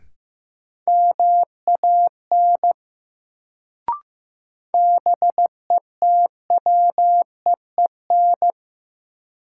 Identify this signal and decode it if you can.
Morse code, heard as a keyed tone, about 15 wpm, 700 Hz. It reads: MAN BETWEEN
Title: Sets of 2 Words - New Word Drill - 15wpm